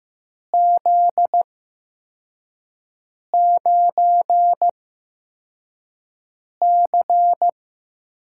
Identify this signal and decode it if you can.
Morse code, heard as a keyed tone, about 15 wpm, spaced as for 4 wpm, 700 Hz. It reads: Z9C
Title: USA Call Signs - 4 Characters - 8x Character Spacing - 15wpm